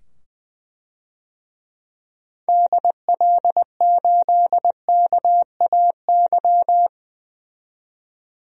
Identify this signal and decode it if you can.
Morse code, heard as a keyed tone, about 20 wpm, 700 Hz. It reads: DL8KAY
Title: DXCC-100 Call Signs - 6 Characters - 20wpm